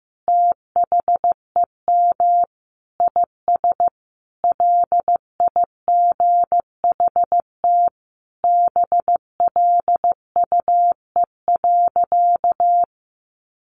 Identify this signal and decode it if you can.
Morse code, heard as a keyed tone, about 15 wpm, 700 Hz. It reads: THEM IS LIGHT BLUE.
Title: Additional Easy Phrases 15wpm